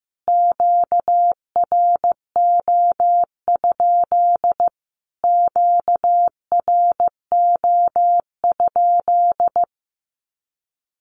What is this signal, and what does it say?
Morse code, heard as a keyed tone, about 15 wpm, 700 Hz. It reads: QRO? QRO?
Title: Q-Codes Repeated 3x 15wpm